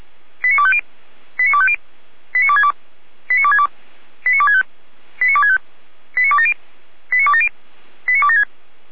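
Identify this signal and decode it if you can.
Five-tone selective call (ZVEI), heard as quick runs of five tones, tones of about 70 ms. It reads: 89250 89250 89262 89262 89265 89265 89270 89270 89276
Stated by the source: ZVEI five-tone call received on SDR.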